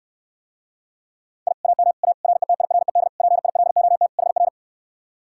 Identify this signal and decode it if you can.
Morse code, heard as a keyed tone, about 68 wpm, 700 Hz. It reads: I AM A RETIRED PILOT HR